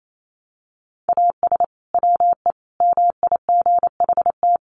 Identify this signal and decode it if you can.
Morse code, heard as a keyed tone, about 28 wpm, 700 Hz. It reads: AS WE MIGHT
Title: Sentences from Top 1100 Words 28wpm